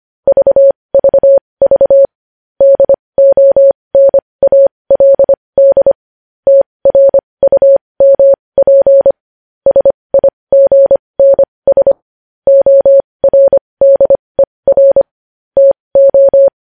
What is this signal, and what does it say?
Morse code, heard as a keyed tone, about 25 wpm, spaced as for 20 wpm, 550 Hz. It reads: VVV DONALD TRUMP SIGNS ORDER TO
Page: News Headlines in Morse Code at 20 WPM